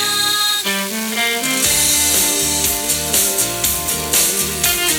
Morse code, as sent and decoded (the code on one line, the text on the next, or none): none